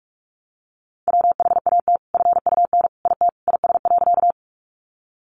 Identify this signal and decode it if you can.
Morse code, heard as a keyed tone, about 44 wpm, 700 Hz. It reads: WHAT FUN IT IS.
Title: Sentences from Top 700 Words 45wpm